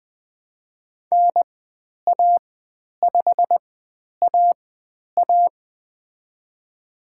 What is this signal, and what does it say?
Morse code, heard as a keyed tone, about 20 wpm, spaced as for 10 wpm, 700 Hz. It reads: NA5AA
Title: 102 Lesson 17g - Letter u - Call Signs 20wpm